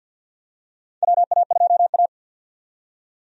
Morse code, heard as a keyed tone, about 52 wpm, 700 Hz. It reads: WA2A
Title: USA Call Signs - 4 Characters - 50wpm